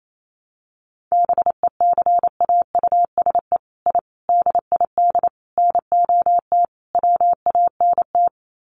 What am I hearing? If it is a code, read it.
Morse code, heard as a keyed tone, about 28 wpm, 700 Hz. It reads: BECAUSE I DID NOT WANT
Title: Sets of 5 Words Encore2 28wpm